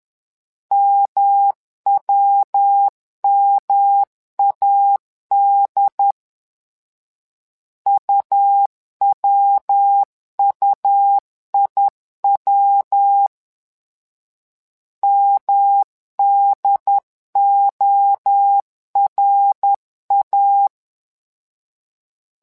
Morse code, heard as a keyed tone, about 10 wpm, 800 Hz. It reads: MWMAD UWUIW MDORA
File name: LEZIONI / 3 MWRO / 3 MWRO 10wpm 2148